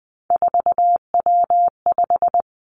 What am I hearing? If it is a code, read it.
Morse code, heard as a keyed tone, about 20 wpm, 700 Hz. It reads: 4W5
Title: Three Challenging Letter-Number Combinations 20wpm